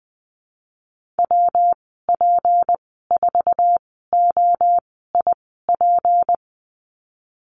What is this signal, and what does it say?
Morse code, heard as a keyed tone, about 20 wpm, spaced as for 15 wpm, 700 Hz. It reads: WP4OIP
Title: 173 Lesson 28g - Number 9 - Call Signs 20wpm